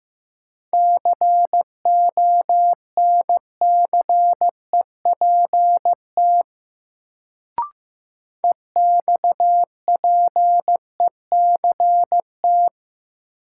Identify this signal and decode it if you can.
Morse code, heard as a keyed tone, about 15 wpm, 700 Hz. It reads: CONCEPT EXPECT
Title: Sentences from Top 1100 Words - New Word Drill Alt - 15wpm